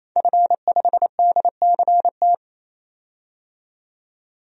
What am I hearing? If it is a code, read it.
Morse code, heard as a keyed tone, about 27 wpm, 700 Hz. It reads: F5DCT
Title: DXCC-100 Call Signs - 5 Characters - 28wpm